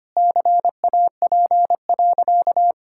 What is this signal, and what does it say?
Morse code, heard as a keyed tone, about 25 wpm, 700 Hz. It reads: CAP.
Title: Sentences from Top 1600 Words 25wpm